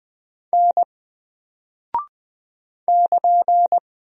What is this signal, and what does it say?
Morse code, heard as a keyed tone, about 20 wpm, 700 Hz. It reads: N <KN>
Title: Single Character (Letter:Number:Punctuation:Prosign) 20wpm